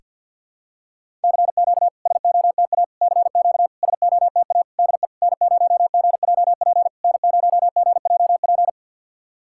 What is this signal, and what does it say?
Morse code, heard as a keyed tone, about 53 wpm, 690 Hz. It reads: CQ SOTA CQ SOTA DE N0GJP N0GJP
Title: SOTA QSO Practice 50wpm